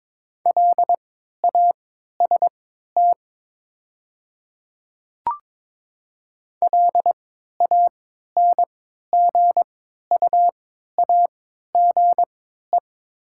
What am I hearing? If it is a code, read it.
Morse code, heard as a keyed tone, about 22 wpm, spaced as for 13 wpm, 700 Hz. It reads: LAST LANGUAGE
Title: 3x Character Spacing - Top 500 Words - 22wpm